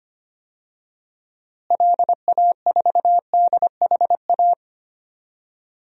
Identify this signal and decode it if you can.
Morse code, heard as a keyed tone, about 25 wpm, 700 Hz. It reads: LA4DHA
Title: DXCC-100 Call Signs - 6 Characters - Repeated 3x - 25wpm